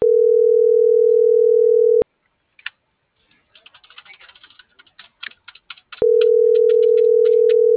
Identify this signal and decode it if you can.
Ringback tone (North America), 440 + 480 Hz.